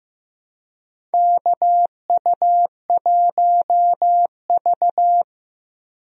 Morse code, heard as a keyed tone, about 15 wpm, 700 Hz. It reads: KU1V
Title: USA Call Signs - 4 Characters - Repeated 3x - 15wpm